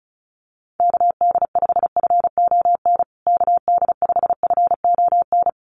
Morse code, heard as a keyed tone, about 34 wpm, 700 Hz. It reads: KD5FON KD5FON
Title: USA Call Signs Repeated 3x 35wpm